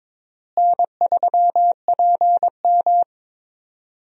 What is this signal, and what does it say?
Morse code, heard as a keyed tone, about 22 wpm, 700 Hz. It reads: N3PM
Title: USA Call Signs - 4 Characters - Repeated 2x - 22wpm